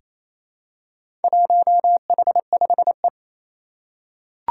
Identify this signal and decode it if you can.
Morse code, heard as a keyed tone, about 27 wpm, 700 Hz. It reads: 1H5E
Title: Four Challenging Letter-Number Combinations 28wpm